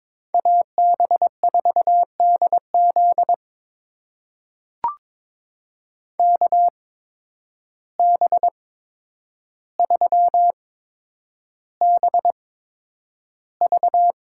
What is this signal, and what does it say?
Morse code, heard as a keyed tone, about 22 wpm, spaced as for 9 wpm, 700 Hz. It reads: AB4DZ KB3BV
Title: USA Call Signs - 5 Characters - 8x Character Spacing - 22wpm